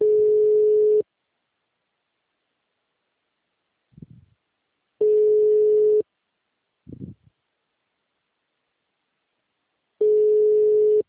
Ringback tone (Europe, 425 Hz).
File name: out- -2011- - - .32